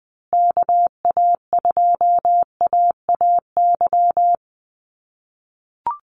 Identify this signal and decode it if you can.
Morse code, heard as a keyed tone, about 20 wpm, 700 Hz. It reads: KA2AAY
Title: USA Call Signs Repeated 3x 20wpm